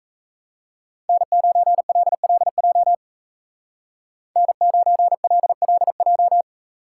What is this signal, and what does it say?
Morse code, heard as a keyed tone, about 34 wpm, 700 Hz. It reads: N9LLJ N9LLJ
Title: Speed Racing - USA 1x3 Call Signs 28wpm